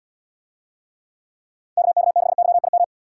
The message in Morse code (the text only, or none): KK6YIM